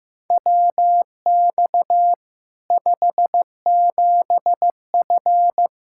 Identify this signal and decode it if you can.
Morse code, heard as a keyed tone, about 16 wpm, 700 Hz. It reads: WX 57F
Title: K1USN SST QSO Practice 15wpm